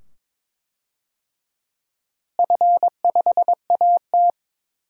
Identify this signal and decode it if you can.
Morse code, heard as a keyed tone, about 22 wpm, 700 Hz. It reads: F5AT
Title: DXCC-100 Call Signs - 4 Characters - Repeated 2x - 22wpm